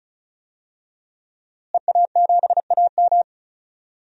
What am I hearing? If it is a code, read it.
Morse code, heard as a keyed tone, about 36 wpm, 700 Hz. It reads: EA7AM
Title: DXCC-100 Call Signs - 5 Characters - Repeated 2x - 35wpm